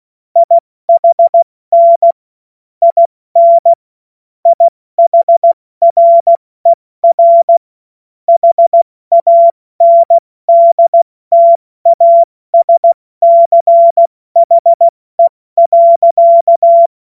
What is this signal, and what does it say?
Morse code, heard as a keyed tone, about 16 wpm, 680 Hz. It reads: IHN IN IHRER HANDTASCHE.
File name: Morsekurs / Witze / Witz 08. November 2022 70 F80